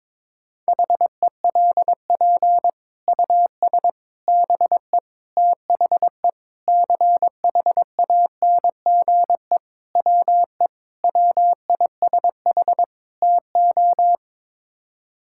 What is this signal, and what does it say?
Morse code, heard as a keyed tone, about 21 wpm, 700 Hz. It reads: HELP US BE THE CHANGE WE WISH TO
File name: practice / 22wpm / Sets of 8 Words - 22wpm